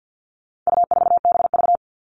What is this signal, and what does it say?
Morse code, heard as a keyed tone, about 50 wpm, 700 Hz. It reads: U4BV
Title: All Cut Numbers - 4-Digit 50wpm